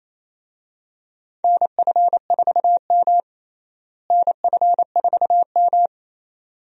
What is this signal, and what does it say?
Morse code, heard as a keyed tone, about 28 wpm, 700 Hz. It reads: NF4M NF4M
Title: USA Call Signs - 4 Characters - Repeated 2x - 28wpm